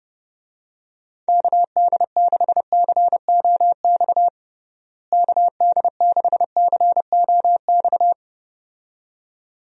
Morse code, heard as a keyed tone, about 30 wpm, 700 Hz. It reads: KD6COX KD6COX
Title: USA Call Signs - 6 Characters - Repeated 2x - 30wpm